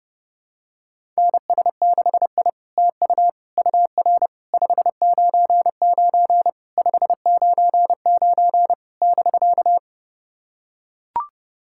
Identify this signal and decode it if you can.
Morse code, heard as a keyed tone, about 30 wpm, 700 Hz. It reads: NS6I TU UR 599 599 <BK>
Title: SOTA QSO Practice 30wpm